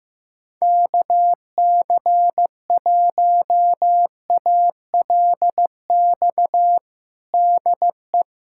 Morse code, heard as a keyed tone, about 16 wpm, 700 Hz. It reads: KC1ALX DE
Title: USA Callsign DE Callsign - 6 Characters - 15wpm